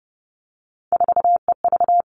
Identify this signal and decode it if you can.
Morse code, heard as a keyed tone, about 30 wpm, 700 Hz. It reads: 4EV
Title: Three Challenging Letter-Number Combinations 30wpm